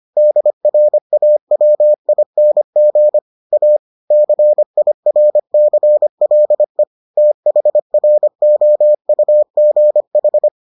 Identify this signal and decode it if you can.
Morse code, heard as a keyed tone, about 25 wpm, 600 Hz. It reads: DRAWING A CIRCLE THROUGH